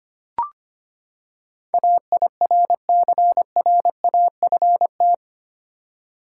Morse code, heard as a keyed tone, about 25 wpm, 700 Hz. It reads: AIRCRAFT
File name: Eight Letter Words - 25wpm